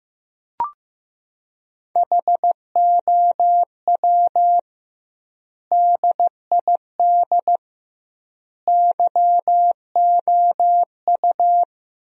Morse code, heard as a keyed tone, about 15 wpm, 700 Hz. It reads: HOW DID YOU
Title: 2x Word-Spacing - Sentences from Top 100 Words - 15wpm